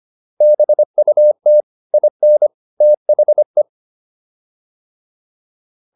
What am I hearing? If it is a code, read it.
Morse code, heard as a keyed tone, about 25 wpm, 600 Hz. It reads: BUT IN THE